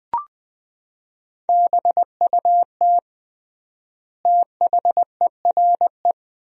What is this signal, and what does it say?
Morse code, heard as a keyed tone, about 20 wpm, 700 Hz. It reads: BUT THERE
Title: 3x Word-Spacing - Sentences from Top 100 Words - 20wpm